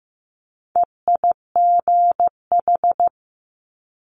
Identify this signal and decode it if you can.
Morse code, heard as a keyed tone, about 15 wpm, 700 Hz. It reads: EIGH
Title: Multi-letter Phonemes Quick Fire 15wpm